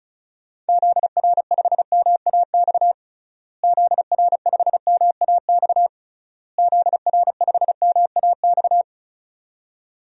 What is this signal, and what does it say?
Morse code, heard as a keyed tone, about 34 wpm, 700 Hz. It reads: ZR5MAX ZR5MAX ZR5MAX
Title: DXCC-100 Call Signs - 6 Characters - Repeated 3x - 35wpm